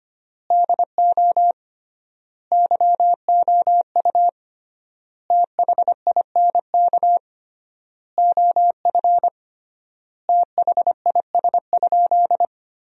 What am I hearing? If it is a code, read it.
Morse code, heard as a keyed tone, about 25 wpm, 700 Hz. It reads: DO YOU THINK OF THIS?
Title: 3x Word-Spacing - Sentences from Top 200 Words - 25wpm